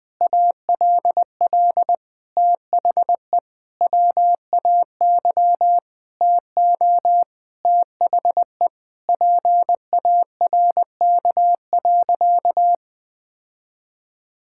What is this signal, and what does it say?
Morse code, heard as a keyed tone, about 20 wpm, 700 Hz. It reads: ALL THE WAY TO THE PARK.
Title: Sentences with 4-Word Phrases - Spoken Beginning - 20wpm